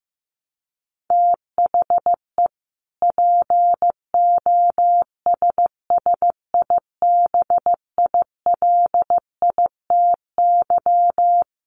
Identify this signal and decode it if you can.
Morse code, heard as a keyed tone, about 15 wpm, 700 Hz. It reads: THE POSSIBILITY
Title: Sets of 3 Words 15wpm